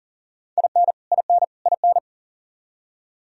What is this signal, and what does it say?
Morse code, heard as a keyed tone, about 40 wpm, 700 Hz. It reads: IN IN IN
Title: Top 100 Words Repeated 40wpm